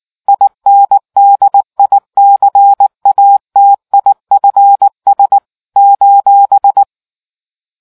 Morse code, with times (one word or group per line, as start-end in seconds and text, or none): 0.3-5.4 INDICATIFS
5.8-6.8 :